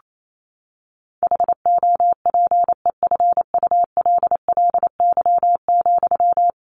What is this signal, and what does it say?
Morse code, heard as a keyed tone, about 28 wpm, 700 Hz. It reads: HOPEFULLY,
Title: Sentences from Top 1500 Words 28wpm